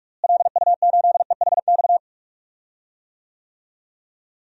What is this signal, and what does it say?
Morse code, heard as a keyed tone, about 46 wpm, 700 Hz. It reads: LU8EHX
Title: DXCC-100 Call Signs - 6 Characters - Repeated 3x - 45wpm